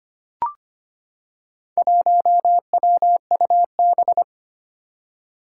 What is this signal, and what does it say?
Morse code, heard as a keyed tone, about 25 wpm, 700 Hz. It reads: 1WUB
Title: Four Challenging Letter-Number Combinations 25wpm